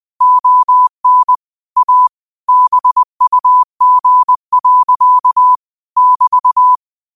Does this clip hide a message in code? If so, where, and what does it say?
Yes: Morse code from 0.2 to 6.8 s, heard as a keyed tone, about 20 wpm, 1000 Hz. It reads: ON A BUG. <BT>